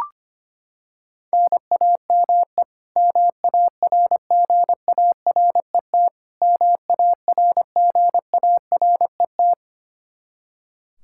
Morse code, heard as a keyed tone, about 25 wpm, 700 Hz. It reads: NAME MARGARET MARGARET
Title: Practice QSO - Exchange 1 - Beginning - 25wpm